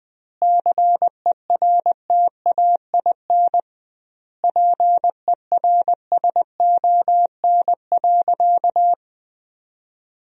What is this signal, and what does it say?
Morse code, heard as a keyed tone, about 20 wpm, 700 Hz. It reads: CERTAIN PERSON.
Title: 2x Word-Spacing - Sentences from Top 500 Words - 20wpm